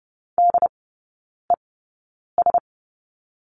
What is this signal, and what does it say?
Morse code, heard as a keyed tone, about 30 wpm, spaced as for 9 wpm, 700 Hz. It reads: DES
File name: 081 Lesson 14e - Number 2 - Three Characters 30wpm